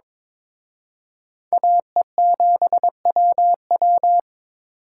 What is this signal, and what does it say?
Morse code, heard as a keyed tone, about 26 wpm, 700 Hz. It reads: AE7WW
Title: Speed Racing - USA 2x2 Call Signs 22wpm